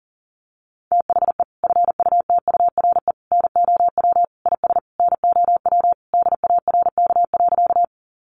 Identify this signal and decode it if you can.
Morse code, heard as a keyed tone, about 40 wpm, 700 Hz. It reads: THE FUTURE NOW IS NOW DARK.
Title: Sentences from Top 700 Words 40wpm